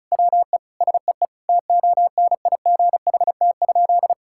Morse code, heard as a keyed tone, about 34 wpm, 700 Hz. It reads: WE SEE TONIGHT?
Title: Sentences from Top 1000 Words - Review Part 5 - 35wpm